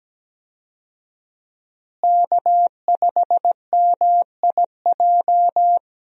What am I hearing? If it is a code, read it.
Morse code, heard as a keyed tone, about 17 wpm, 700 Hz. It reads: K5MIJ
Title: USA Call Signs - 5 Characters - Repeated 3x - 17wpm